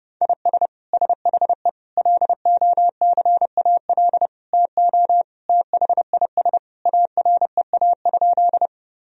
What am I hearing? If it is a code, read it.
Morse code, heard as a keyed tone, about 30 wpm, 700 Hz. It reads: IS SHE LOCAL TO THIS AREA?
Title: Sentences from Top 1000 Words - Review Part 3 - 30wpm